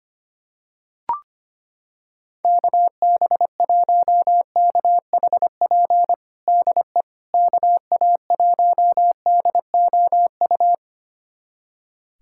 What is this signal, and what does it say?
Morse code, heard as a keyed tone, about 25 wpm, 700 Hz. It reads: KB1KHP DE KA1DOU
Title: USA Callsign DE Callsign 25wpm